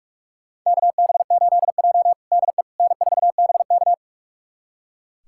Morse code, heard as a keyed tone, about 44 wpm, 700 Hz. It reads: KB8J DE N4BK
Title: USA Callsign DE Callsign - 4 Characters - 45wpm